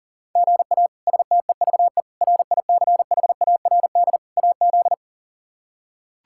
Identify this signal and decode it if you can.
Morse code, heard as a keyed tone, about 40 wpm, 700 Hz. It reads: GA STEVE RICHARD AZ
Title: K1USN SST QSO Practice - 40wpm